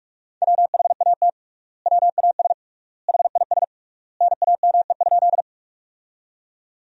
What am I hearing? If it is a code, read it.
Morse code, heard as a keyed tone, about 46 wpm, 700 Hz. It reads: WHAT WAS HIS NAME?
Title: 3x Word-Spacing - Sentences from Top 200 Words - 45wpm